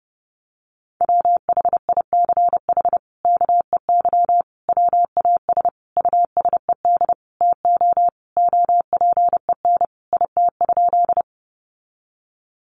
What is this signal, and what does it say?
Morse code, heard as a keyed tone, about 30 wpm, 700 Hz. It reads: WHICH KEY WAS USED TO OPEN IT?
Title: Sentences from Top 600 Words 30wpm